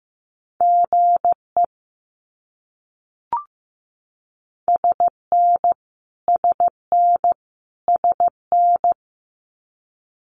Morse code, heard as a keyed tone, about 15 wpm, 700 Hz. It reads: GE SN SN SN
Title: Abbreviations Repeated 3x - 15wpm